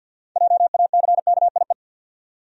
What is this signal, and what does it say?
Morse code, heard as a keyed tone, about 51 wpm, 700 Hz. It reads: JACKIE
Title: USA Common Names - 50wpm